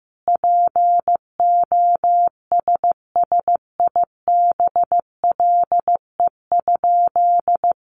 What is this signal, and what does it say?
Morse code, heard as a keyed tone, about 15 wpm, 700 Hz. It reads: POSSIBLE?
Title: Sentences from Top 1500 Words 15wpm